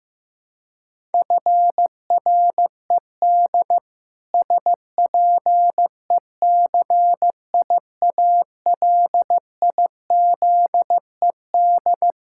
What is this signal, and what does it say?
Morse code, heard as a keyed tone, about 15 wpm, 700 Hz. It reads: FRED SPECIALIZED
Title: Pangram Challenge v2 - 15wpm